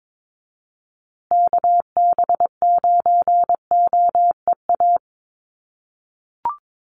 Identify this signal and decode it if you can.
Morse code, heard as a keyed tone, about 22 wpm, 700 Hz. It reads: KB9OEA
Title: USA Call Signs Repeated 2x 22wpm